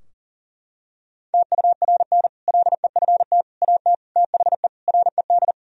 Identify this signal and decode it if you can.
Morse code, heard as a keyed tone, about 40 wpm, 700 Hz. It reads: TURN LEFT AT THE RED
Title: Sentences from Top 1000 Words - Review Part 2 - 40wpm